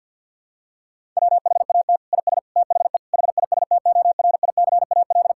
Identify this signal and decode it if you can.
Morse code, heard as a keyed tone, about 50 wpm, 700 Hz. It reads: WHAT IS THE HISTORICAL
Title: Sentences from Top 1000 Words - Review Part 5 - 50wpm